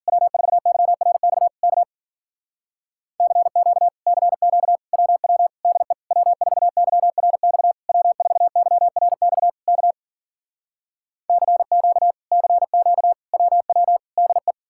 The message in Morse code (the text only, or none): W4YRX K CQ CQ WW DE W4YRX W4YRX K CQ CQ WW DE